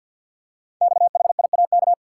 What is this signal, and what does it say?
Morse code, heard as a keyed tone, about 52 wpm, 700 Hz. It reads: XHIAK
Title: Code Groups - 5 Letter:Numbers - 50wpm